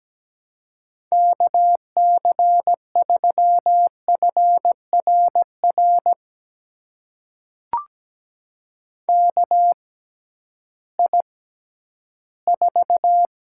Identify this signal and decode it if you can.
Morse code, heard as a keyed tone, about 17 wpm, spaced as for 11 wpm, 700 Hz. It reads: KC3FRR KI4
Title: USA Call Signs - 6 Characters - 6x Character Spacing - 17wpm